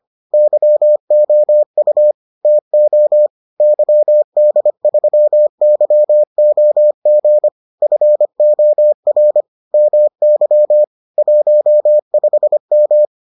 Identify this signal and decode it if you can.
Morse code, heard as a keyed tone, about 25 wpm, 600 Hz. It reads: YOU TO YD3YOG FOR MY 15M